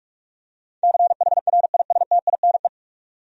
Morse code, heard as a keyed tone, about 46 wpm, 700 Hz. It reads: CHRISTINE
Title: USA Common Names - 45wpm